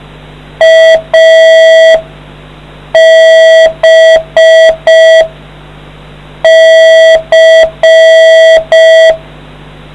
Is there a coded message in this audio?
Yes: Morse code from 0.6 to 9.1 s, heard as a keyed tone, about 5 wpm, 650 Hz, with noise. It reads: A B C